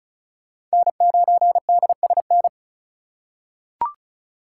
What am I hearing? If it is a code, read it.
Morse code, heard as a keyed tone, about 36 wpm, 700 Hz. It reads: N9DSN
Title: USA Call Signs - 5 Characters - 2x Character Spacing - 35wpm